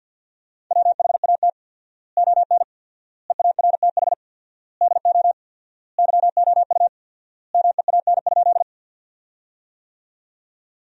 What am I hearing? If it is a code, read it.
Morse code, heard as a keyed tone, about 50 wpm, 700 Hz. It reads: WHAT ON EARTH DO YOU MEAN?
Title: 4x Word-Spacing - Sentences from Top 200 Words - 50wpm